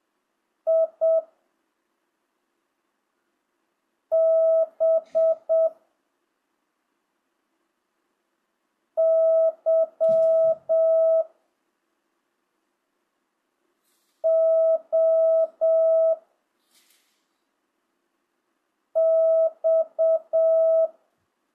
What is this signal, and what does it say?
Morse code, heard as a keyed tone, about 7 wpm, 1300 Hz. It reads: I B Y O X
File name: jh8cbh / 2025cw